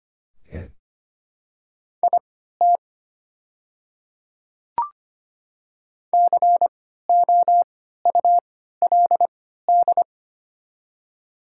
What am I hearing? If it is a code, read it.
Morse code, heard as a keyed tone, about 25 wpm, spaced as for 14 wpm, 700 Hz. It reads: IT COULD
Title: 3x Character Spacing - Top 200 Words - 25wpm